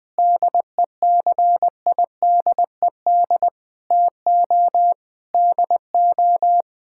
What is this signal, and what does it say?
Morse code, heard as a keyed tone, about 20 wpm, 700 Hz. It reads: DECIDED TO DO
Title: Sentences with 3-Word Phrases - 20wpm